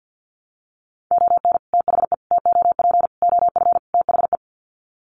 Morse code, heard as a keyed tone, about 50 wpm, 700 Hz. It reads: ON THE TOP OF THE
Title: Sets of 5 Words Encore 50wpm